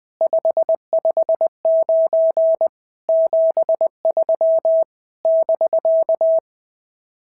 Morse code, heard as a keyed tone, about 19 wpm, 650 Hz. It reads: 559 73 <BK>
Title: SOTA QSO Practice 20wpm